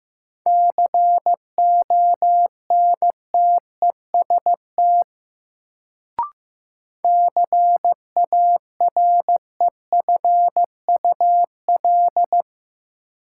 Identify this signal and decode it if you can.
Morse code, heard as a keyed tone, about 15 wpm, 700 Hz. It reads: CONTEST CAREFUL
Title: Sentences from Top 1600 Words - New Word Drill Alt - 15wpm